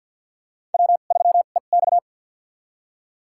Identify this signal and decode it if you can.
Morse code, heard as a keyed tone, about 52 wpm, spaced as for 37 wpm, 700 Hz. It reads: W3EX
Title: USA Call Signs - 4 Characters - 2x Character Spacing - 50wpm